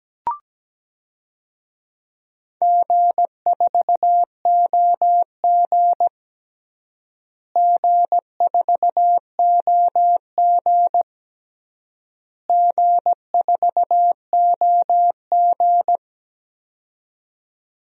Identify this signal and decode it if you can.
Morse code, heard as a keyed tone, about 17 wpm, 700 Hz. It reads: G4OG G4OG G4OG
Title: DXCC-100 Call Signs - 4 Characters - Repeated 3x - 17wpm